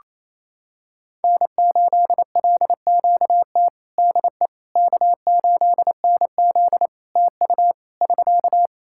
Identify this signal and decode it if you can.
Morse code, heard as a keyed tone, about 28 wpm, 700 Hz. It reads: N8LQT DE K8NZ TU <SK>
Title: Practice QSO - Exchange 4 - Ending - 28wpm